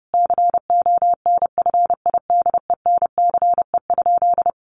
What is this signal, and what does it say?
Morse code, heard as a keyed tone, about 30 wpm, 700 Hz. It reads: CONFIDENCE?
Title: Sentences from Top 1500 Words 30wpm